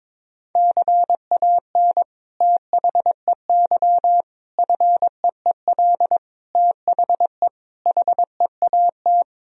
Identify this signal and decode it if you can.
Morse code, heard as a keyed tone, about 22 wpm, 700 Hz. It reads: CAN THEY FEEL THE HEAT
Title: Sentences from Top 1000 Words - Review Part 3 - 22wpm